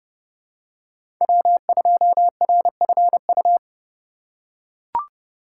Morse code, heard as a keyed tone, about 30 wpm, 700 Hz. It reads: W2RFU
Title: USA Call Signs - 5 Characters - 4x Character Spacing - 30wpm